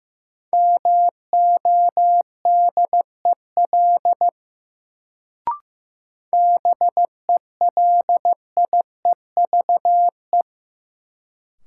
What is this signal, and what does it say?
Morse code, heard as a keyed tone, about 15 wpm, 700 Hz. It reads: MODEL BELIEVE
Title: Sentences from Top 1000 Words - New Word Drill - 15wpm